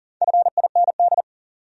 Morse code, heard as a keyed tone, about 49 wpm, 700 Hz. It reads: FIND
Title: Speed-Racing Top 300 Words 40wpm